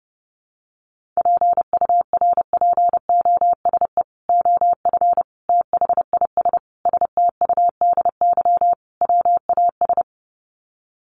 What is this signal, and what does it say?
Morse code, heard as a keyed tone, about 30 wpm, 700 Hz. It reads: PURPOSE OF THIS STUDY WAS
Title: Sets of 5 Words 30wpm